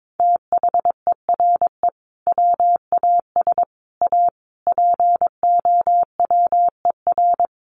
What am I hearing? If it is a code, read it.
Morse code, heard as a keyed tone, about 22 wpm, 700 Hz. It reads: THERE WAS A POWER
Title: Sentences from Top 800 Words 22wpm